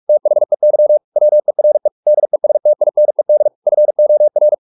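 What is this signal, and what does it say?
Morse code, heard as a keyed tone, about 44 wpm, 600 Hz. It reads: THEY WERE DESTINED FOR
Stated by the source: Morse practice at 45 wpm, the novel Never by Ken Follett in English, from CW med Gnister.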